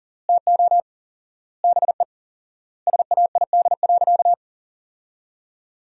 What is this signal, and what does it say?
Morse code, heard as a keyed tone, about 40 wpm, 700 Hz. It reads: TO BE SAID.
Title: 4x Word-Spacing - Sentences from Top 100 Words - 40wpm